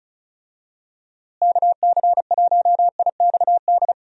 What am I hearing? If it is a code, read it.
Morse code, heard as a keyed tone, about 34 wpm, 700 Hz. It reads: KC1IXD
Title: USA Call Signs - 6 Characters - Repeated 3x - 35wpm